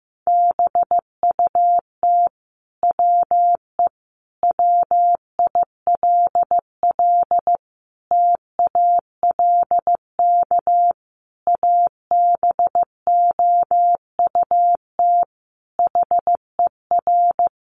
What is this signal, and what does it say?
Morse code, heard as a keyed tone, about 15 wpm, 700 Hz. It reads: BUT WE WILL TALK ABOUT HER
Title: Sentences from Top 1200 Words 15wpm